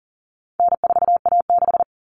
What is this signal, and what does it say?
Morse code, heard as a keyed tone, about 40 wpm, 700 Hz. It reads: N4A6
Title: Common Cut Numbers - 4-Digit 40wpm